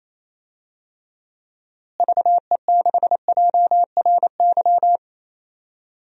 Morse code, heard as a keyed tone, about 28 wpm, 700 Hz. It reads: VE6JRY
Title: DXCC-100 Call Signs - 6 Characters - Repeated 3x - 28wpm